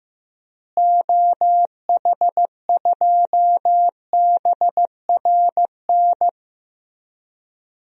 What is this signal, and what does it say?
Morse code, heard as a keyed tone, about 15 wpm, 700 Hz. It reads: OH2BRN
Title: DXCC-100 Call Signs - 6 Characters - Repeated 3x - 15wpm